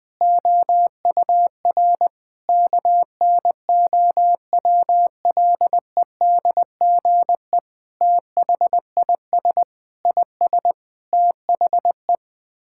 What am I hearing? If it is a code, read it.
Morse code, heard as a keyed tone, about 20 wpm, 700 Hz. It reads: OUR KNOWLEDGE THIS IS THE
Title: Sets of 9 Words - 20wpm